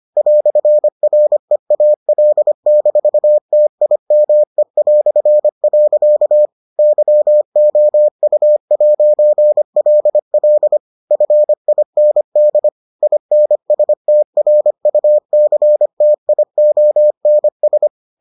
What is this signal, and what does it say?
Morse code, heard as a keyed tone, about 25 wpm, 600 Hz. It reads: "REAL-TIME". YOU'LL FIND INSTRUCTIONS